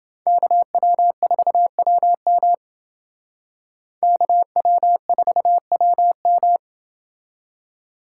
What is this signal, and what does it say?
Morse code, heard as a keyed tone, about 28 wpm, 700 Hz. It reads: KW4WM KW4WM